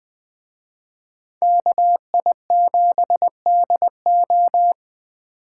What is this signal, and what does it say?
Morse code, heard as a keyed tone, about 20 wpm, 700 Hz. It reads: KI7DO